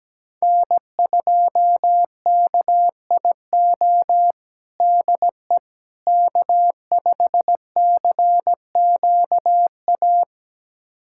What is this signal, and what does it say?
Morse code, heard as a keyed tone, about 17 wpm, 700 Hz. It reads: N2KIO DE K5CQA
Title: USA Callsign DE Callsign - 5 Characters - 17wpm